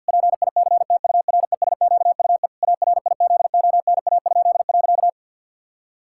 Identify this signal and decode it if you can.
Morse code, heard as a keyed tone, about 48 wpm, 700 Hz. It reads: PICTURESQUE ARIZONA?.
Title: Pangram Challenge v2 - 50wpm